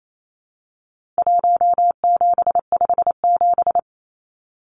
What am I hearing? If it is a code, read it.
Morse code, heard as a keyed tone, about 28 wpm, 700 Hz. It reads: 1757
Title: Numbers - Contest Style - 4-Digit 28wpm